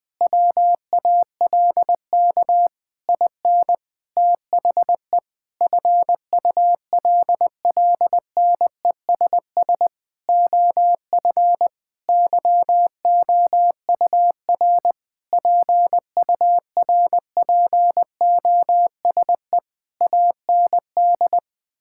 Morse code, heard as a keyed tone, about 21 wpm, 700 Hz. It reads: WALK IN THE FULLNESS OF YOUR PURPOSE AND
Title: Sentences with 9-Word Phrases - 20wpm